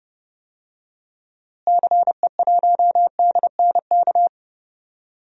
Morse code, heard as a keyed tone, about 30 wpm, 700 Hz. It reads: CE1DNK
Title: DXCC-100 Call Signs - 6 Characters - Repeated 3x - 30wpm